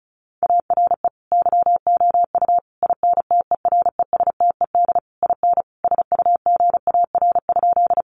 Morse code, heard as a keyed tone, about 34 wpm, 700 Hz. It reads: ARE YOU INTERESTED IN SUGAR?
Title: Sentences from Top 1200 Words 35wpm